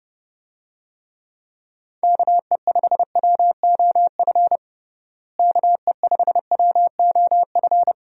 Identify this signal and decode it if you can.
Morse code, heard as a keyed tone, about 30 wpm, 700 Hz. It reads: KE5WOF KE5WOF
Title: USA Call Signs - 6 Characters - Repeated 3x - 30wpm